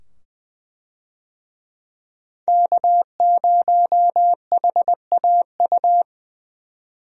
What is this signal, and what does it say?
Morse code, heard as a keyed tone, about 20 wpm, 700 Hz. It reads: K0HAU
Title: USA Call Signs - 5 Characters - 20wpm